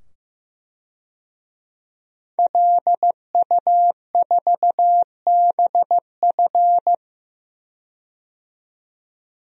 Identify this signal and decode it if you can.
Morse code, heard as a keyed tone, about 15 wpm, 700 Hz. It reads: LU4BF
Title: DXCC-100 Call Signs - 5 Characters - 15wpm